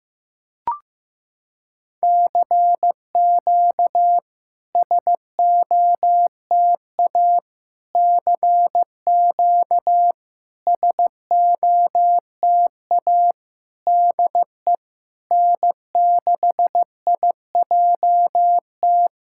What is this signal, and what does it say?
Morse code, heard as a keyed tone, about 14 wpm, 690 Hz. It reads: CQ SOTA CQ SOTA DE N6IJT
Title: SOTA QSO Practice 15wpm